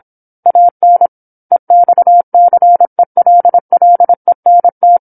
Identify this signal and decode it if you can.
Morse code, heard as a keyed tone, about 27 wpm, 700 Hz. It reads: AN EXCELLENT
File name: 08-Chapter-08 26 WPM